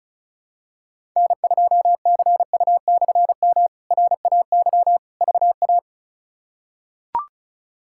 Morse code, heard as a keyed tone, about 36 wpm, 700 Hz. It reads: N2CU/M RAY VA
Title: K1USN SST QSO Practice - 35wpm